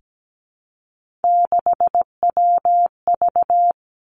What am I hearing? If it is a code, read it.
Morse code, heard as a keyed tone, about 17 wpm, 700 Hz. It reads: 6WV
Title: Three Challenging Letter-Number Combinations 17wpm